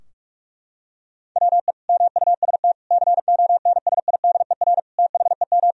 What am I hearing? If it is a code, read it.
Morse code, heard as a keyed tone, about 46 wpm, 700 Hz. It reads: WE MUST CONSIDER THEM
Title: Sentences from Top 1000 Words - Review Part 4 - 45wpm